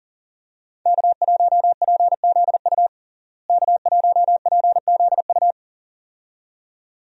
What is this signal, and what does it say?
Morse code, heard as a keyed tone, about 40 wpm, 700 Hz. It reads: K1PZU K1PZU
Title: USA Call Signs - 5 Characters - Repeated 2x - 40wpm